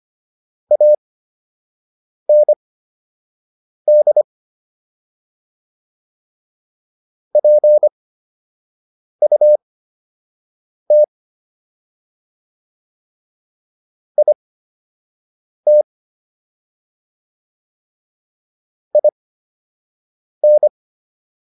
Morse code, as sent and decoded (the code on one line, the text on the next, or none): .- -. -.. / .--. ..- - / .. - / .. -.
AND PUT IT IN